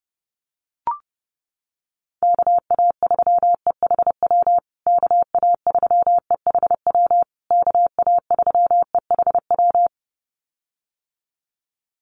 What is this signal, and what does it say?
Morse code, heard as a keyed tone, about 30 wpm, 700 Hz. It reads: KA3EHW KA3EHW KA3EHW
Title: USA Call Signs Repeated 3x 30wpm